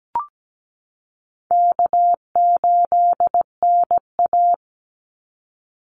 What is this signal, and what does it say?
Morse code, heard as a keyed tone, about 17 wpm, 700 Hz. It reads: K8NA
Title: Easy Call Signs Repeated 3x - 17wpm